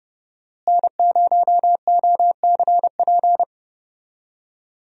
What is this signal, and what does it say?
Morse code, heard as a keyed tone, about 30 wpm, 700 Hz. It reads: N0OCP
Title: USA Call Signs - 5 Characters - Repeated 3x - 30wpm